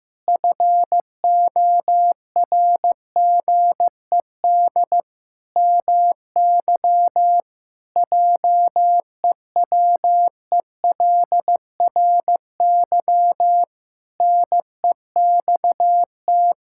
Morse code, heard as a keyed tone, about 15 wpm, 700 Hz. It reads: FORGED MY JEWELRY NEXT
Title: Pangram Challenge v2 - 15wpm